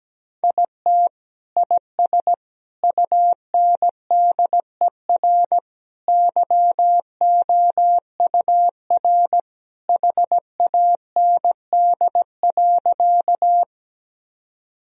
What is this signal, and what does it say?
Morse code, heard as a keyed tone, about 17 wpm, 700 Hz. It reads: IT IS UNDER YOUR HAND.
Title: Sentences from Top 1000 Words - Review Part 1 - 17wpm